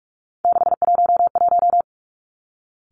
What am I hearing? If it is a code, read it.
Morse code, heard as a keyed tone, about 46 wpm, 700 Hz. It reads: 611
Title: Three Challenging Letter-Number Combinations 45wpm